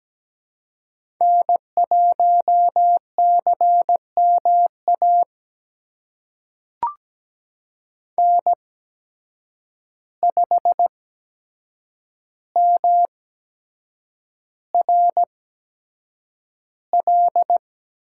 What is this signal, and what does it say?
Morse code, heard as a keyed tone, about 17 wpm, spaced as for 7 wpm, 700 Hz. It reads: N1CMA N5MRL
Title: USA Call Signs - 5 Characters - 8x Character Spacing - 17wpm